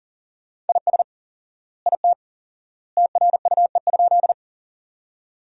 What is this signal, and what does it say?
Morse code, heard as a keyed tone, about 40 wpm, 700 Hz. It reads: IS IT TRUE?
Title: 4x Word-Spacing - Sentences from Top 400 Words - 40wpm